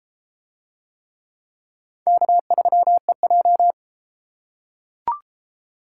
Morse code, heard as a keyed tone, about 28 wpm, 700 Hz. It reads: K3EJ